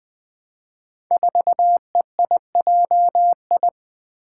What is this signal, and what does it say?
Morse code, heard as a keyed tone, about 20 wpm, 700 Hz. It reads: 4EIJI
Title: Five Challenging Letter-Number Combinations 20wpm